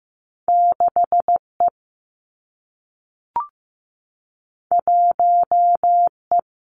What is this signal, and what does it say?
Morse code, heard as a keyed tone, about 15 wpm, 700 Hz. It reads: 6 E 1 E
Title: Two Challenging Letter-Number Combinations 15wpm